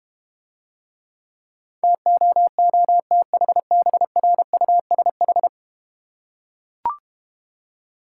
Morse code, heard as a keyed tone, about 31 wpm, 700 Hz. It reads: TOOTHBRUSH
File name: Easy Words - Speed-Racing with Hints - 25wpm